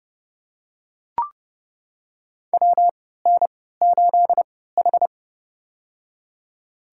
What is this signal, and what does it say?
Morse code, heard as a keyed tone, about 30 wpm, spaced as for 17 wpm, 700 Hz. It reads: WN8H
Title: USA Call Signs - 4 Characters - 3x Character Spacing - 30wpm